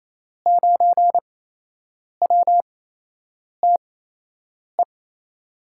Morse code, heard as a keyed tone, about 27 wpm, spaced as for 8 wpm, 700 Hz. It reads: 9WTE
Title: USA Call Signs - 6 Characters - 8x Character Spacing - 28wpm